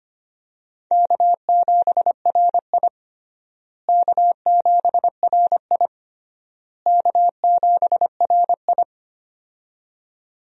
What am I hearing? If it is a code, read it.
Morse code, heard as a keyed tone, about 25 wpm, 700 Hz. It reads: K7RI K7RI K7RI
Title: USA Call Signs - 4 Characters - Repeated 3x - 25wpm